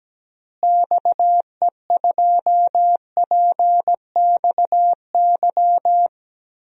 Morse code, heard as a keyed tone, about 17 wpm, 700 Hz. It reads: XE2PXY
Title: DXCC-100 Call Signs - 6 Characters - Repeated 3x - 17wpm